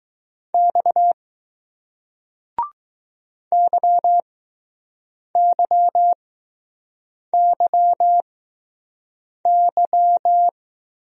Morse code, heard as a keyed tone, about 20 wpm, spaced as for 6 wpm, 700 Hz. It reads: X YYYY